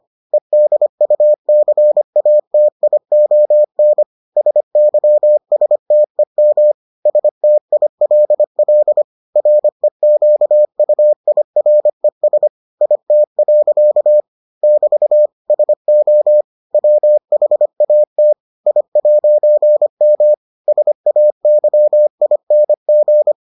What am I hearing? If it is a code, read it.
Morse code, heard as a keyed tone, about 25 wpm, 600 Hz. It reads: EDUCATION SYSTEM STILL REQUIRES IT. <BT> SO WHAT I'M SAYING